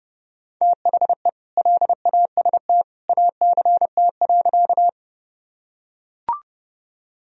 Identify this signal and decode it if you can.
Morse code, heard as a keyed tone, about 31 wpm, 700 Hz. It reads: THE LAST ACT.
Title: Sentences from Top 1000 Words - Review Part 1 - 30wpm